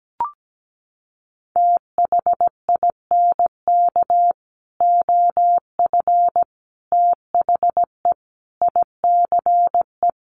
Morse code, heard as a keyed tone, about 17 wpm, 700 Hz. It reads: THINK OF THE ICE
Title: Sentences from Top 1500 Words 17wpm